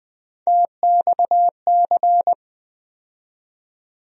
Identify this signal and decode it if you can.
Morse code, heard as a keyed tone, about 20 wpm, 700 Hz. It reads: TXC
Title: USA Call Sign Postfixes - 3 Letters 20wpm